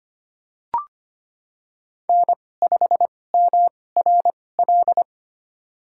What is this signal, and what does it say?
Morse code, heard as a keyed tone, about 25 wpm, spaced as for 18 wpm, 700 Hz. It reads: N5MRL
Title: USA Call Signs - 5 Characters - 2x Character Spacing - 25wpm